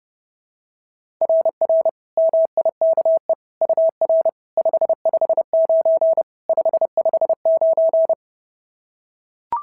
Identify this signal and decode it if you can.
Morse code, heard as a keyed tone, about 26 wpm, 650 Hz. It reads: RR MIKE UR 559 559
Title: SOTA QSO Practice 30wpm